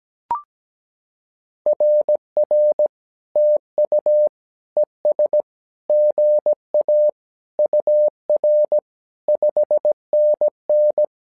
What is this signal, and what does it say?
Morse code, heard as a keyed tone, about 17 wpm, 600 Hz. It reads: RR TU ES GA UR 5NN
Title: SOTA QSO Practice 17wpm